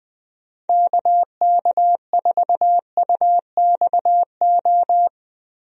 Morse code, heard as a keyed tone, about 20 wpm, 700 Hz. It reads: KK4UXO
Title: USA Call Signs - 6 Characters - Repeated 3x - 20wpm